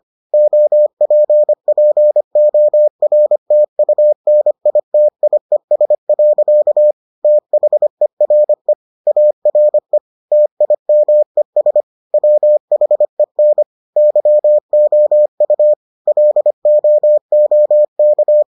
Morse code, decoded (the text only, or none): OPPORTUNITIES. THERE ARE TIMES WHEN YOU LOOK